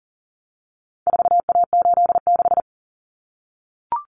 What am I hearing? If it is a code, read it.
Morse code, heard as a keyed tone, about 40 wpm, 700 Hz. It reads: 4A86
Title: Common Cut Numbers - 4-Digit 40wpm